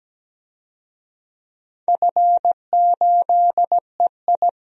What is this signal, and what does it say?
Morse code, heard as a keyed tone, about 17 wpm, 700 Hz. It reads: F8EI
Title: DXCC-100 Call Signs - 4 Characters - 17wpm